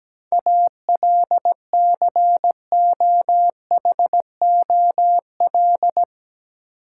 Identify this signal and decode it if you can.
Morse code, heard as a keyed tone, about 17 wpm, 700 Hz. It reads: ALCOHOL
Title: Sentences from Top 1400 Words - New Word Drill Alt - 17wpm